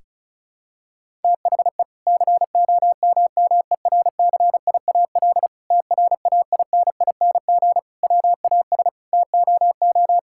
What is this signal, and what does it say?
Morse code, heard as a keyed tone, about 36 wpm, 700 Hz. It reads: THE COMMERCIAL TRAINING WAS TOO
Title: Sentences from Top 1000 Words - Review Part 5 - 35wpm